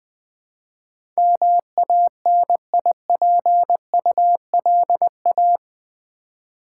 Morse code, heard as a keyed tone, about 20 wpm, 700 Hz. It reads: MANIPULA
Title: Build A Word - 4 Letters At A Time - 12-Letter Words - 20wpm